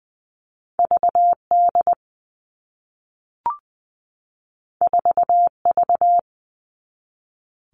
Morse code, heard as a keyed tone, about 20 wpm, 700 Hz. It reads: VD 4V
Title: All Cut Numbers - 2-Digit 20wpm